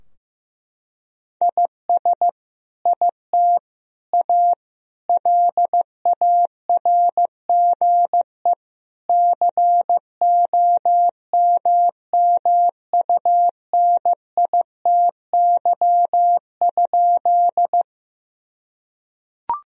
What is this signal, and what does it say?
Morse code, heard as a keyed tone, about 15 wpm, 700 Hz. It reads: IS IT A LARGE COMMUNITY?
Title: Sentences from Top 1000 Words - Review Part 4 - 15wpm